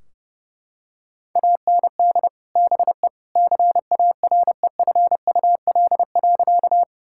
Morse code, heard as a keyed tone, about 30 wpm, 700 Hz. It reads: AND BE CAREFUL.
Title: Sentences from Top 1600 Words 30wpm